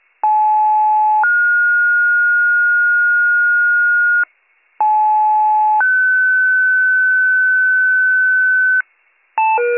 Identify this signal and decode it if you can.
Two-tone page, heard as a one-second tone, then a three-second tone, about 860 then 1430 Hz, about 860 then 1500 Hz. It is heard from 0.2 to 8.8 s.